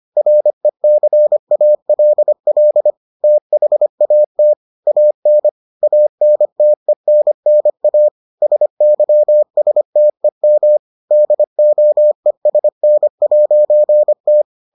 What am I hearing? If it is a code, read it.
Morse code, heard as a keyed tone, about 25 wpm, 600 Hz. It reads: RECALL THAT AN ANTENNA SYSTEM DOESN'T